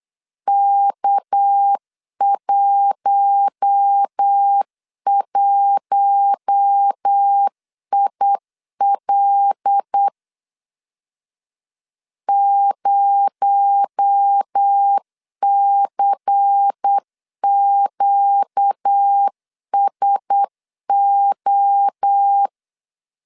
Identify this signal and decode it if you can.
Morse code, heard as a keyed tone, about 8 wpm, 800 Hz. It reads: K11IL 0CQSO
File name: CW / DET / 1 6 1500C S C 8wpm 5219